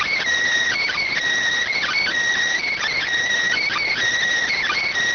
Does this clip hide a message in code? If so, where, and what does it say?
Yes: Emergency Alert System (SAME) tones from 0.8 to 4.9 s, heard as harsh buzzing bursts.